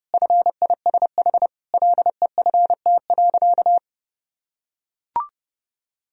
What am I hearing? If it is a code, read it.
Morse code, heard as a keyed tone, about 30 wpm, 700 Hz. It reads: FISH LEFT.
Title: Sentences from Top 1000 Words - Review Part 1 - 30wpm